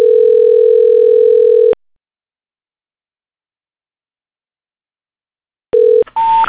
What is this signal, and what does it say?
Ringback tone (North America), 440 + 480 Hz.
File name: recordings / RE a20bea08d c